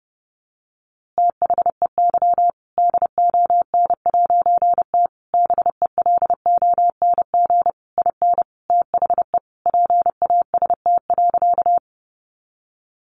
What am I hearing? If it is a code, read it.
Morse code, heard as a keyed tone, about 30 wpm, 700 Hz. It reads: THEY DON'T BELONG IN THE PAST.
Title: A Sound Of Thunder 30wpm